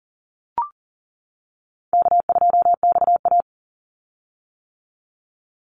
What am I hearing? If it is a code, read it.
Morse code, heard as a keyed tone, about 40 wpm, 700 Hz. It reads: K2XA
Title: Call Signs - 40wpm